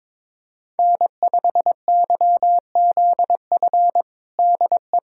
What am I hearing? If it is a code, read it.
Morse code, heard as a keyed tone, about 22 wpm, 700 Hz. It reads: N5YZF DE
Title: USA Callsign DE Callsign - 5 Characters - 22wpm